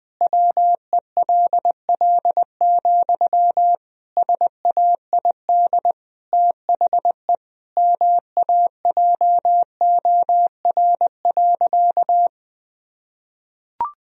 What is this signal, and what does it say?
Morse code, heard as a keyed tone, about 20 wpm, 700 Hz. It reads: WELL, SAID THE MAJOR.
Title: Sentences from Top 1000 Words - Review Part 4 - 20wpm